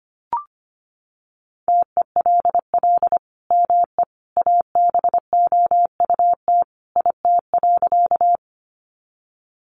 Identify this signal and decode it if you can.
Morse code, heard as a keyed tone, about 25 wpm, 700 Hz. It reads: TELL ME ABOUT IT.
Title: Sentences from Top 200 Words 25wpm